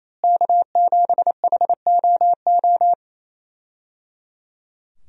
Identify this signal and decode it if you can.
Morse code, heard as a keyed tone, about 27 wpm, 700 Hz. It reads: K7HOO
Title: USA Call Signs - 5 Characters - Repeated 3x - 28wpm